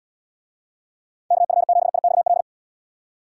Morse code, heard as a keyed tone, about 75 wpm, 700 Hz. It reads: KF7IXL